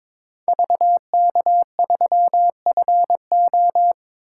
Morse code, heard as a keyed tone, about 26 wpm, 700 Hz. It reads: VK3FO
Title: Speed Racing - USA 2x2 Call Signs 22wpm